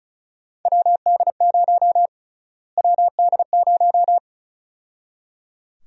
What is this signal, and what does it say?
Morse code, heard as a keyed tone, about 34 wpm, 700 Hz. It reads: WD0 WD0
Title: USA Call Sign Prefixes - Repeated 2x - 35wpm